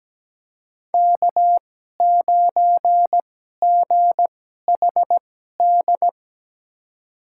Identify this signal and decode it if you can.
Morse code, heard as a keyed tone, about 17 wpm, spaced as for 12 wpm, 700 Hz. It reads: K9GHD
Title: USA Call Signs - 5 Characters - 2x Character Spacing - 17wpm